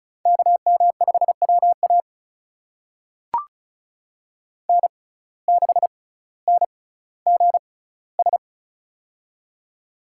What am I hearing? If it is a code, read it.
Morse code, heard as a keyed tone, about 34 wpm, spaced as for 18 wpm, 700 Hz. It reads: KM5WA N6NGS
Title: USA Call Signs - 5 Characters - 6x Character Spacing - 35wpm